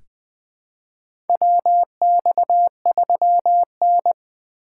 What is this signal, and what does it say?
Morse code, heard as a keyed tone, about 20 wpm, 700 Hz. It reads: WX3N
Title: USA Call Signs - 4 Characters - 8x Character Spacing - 20wpm